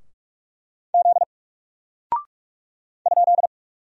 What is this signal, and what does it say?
Morse code, heard as a keyed tone, about 44 wpm, 700 Hz. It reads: Z ?
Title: Single Character (Letter:Number:Punctuation:Prosign) 45wpm